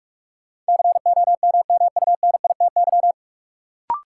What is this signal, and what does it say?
Morse code, heard as a keyed tone, about 46 wpm, 700 Hz. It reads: COMMUNITY
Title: Sentences from Top 1000 Words - Review Part 4 - New Word Drill - 45wpm